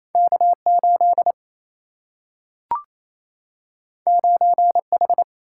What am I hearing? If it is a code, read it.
Morse code, heard as a keyed tone, about 28 wpm, 700 Hz. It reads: K8 9H
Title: Single-Letter Single-Number 28wpm